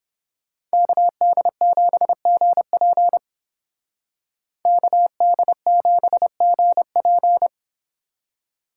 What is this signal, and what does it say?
Morse code, heard as a keyed tone, about 31 wpm, 700 Hz. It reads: KD7GP KD7GP